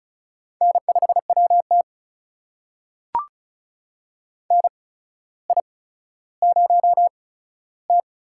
Morse code, heard as a keyed tone, about 36 wpm, spaced as for 15 wpm, 700 Hz. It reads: N5WT NI0T
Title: USA Call Signs - 4 Characters - 8x Character Spacing - 35wpm